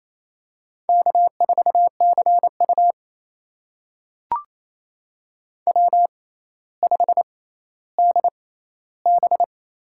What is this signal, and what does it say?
Morse code, heard as a keyed tone, about 27 wpm, spaced as for 14 wpm, 700 Hz. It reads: K4CU W5DB
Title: USA Call Signs - 6x Character Spacing - 28wpm